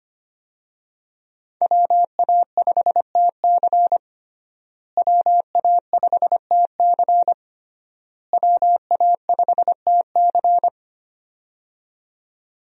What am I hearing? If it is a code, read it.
Morse code, heard as a keyed tone, about 25 wpm, 700 Hz. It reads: WA5TC WA5TC WA5TC
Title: USA Call Signs - 5 Characters - Repeated 3x - 25wpm